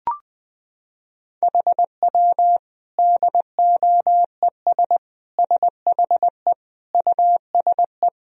Morse code, heard as a keyed tone, about 20 wpm, 700 Hz. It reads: HW DOES SHE USE
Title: Sentences from Top 200 Words and Abbreviations 20wpm